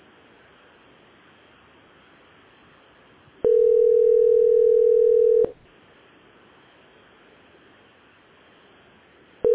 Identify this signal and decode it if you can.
Ringback tone (North America), 440 + 480 Hz.